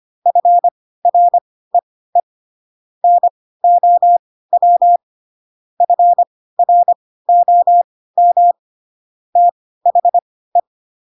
Morse code, heard as a keyed tone, about 25 wpm, spaced as for 16 wpm, 700 Hz. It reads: FREE NOW FROM THE
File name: Pinocchio / Pinocchio ch4 10 25 ik1hge